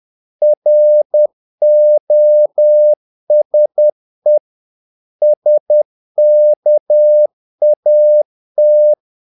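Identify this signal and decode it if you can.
Morse code, heard as a keyed tone, about 10 wpm, 600 Hz. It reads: ROSE SKAT
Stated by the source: Morse practice of short Danish words at 10 wpm, from CW med Gnister.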